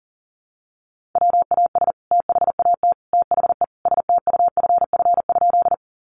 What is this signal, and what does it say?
Morse code, heard as a keyed tone, about 40 wpm, 700 Hz. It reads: WAS THAT THE STUFF?
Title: Sentences from Top 1400 Words 40wpm